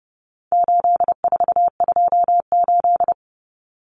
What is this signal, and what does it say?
Morse code, heard as a keyed tone, about 30 wpm, 700 Hz. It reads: 8428
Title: Numbers - 4-Digit 30wpm